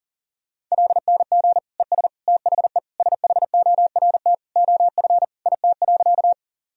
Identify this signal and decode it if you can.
Morse code, heard as a keyed tone, about 40 wpm, 700 Hz. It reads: LNG ES THE SHORT OF IT.
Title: Sentences from Top 400 Words and Abbreviations 40wpm